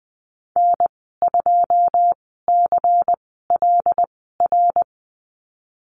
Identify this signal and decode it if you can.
Morse code, heard as a keyed tone, about 20 wpm, spaced as for 14 wpm, 700 Hz. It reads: N2CLR
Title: 128 Lesson 21g - Number 6 - Call Signs 20wpm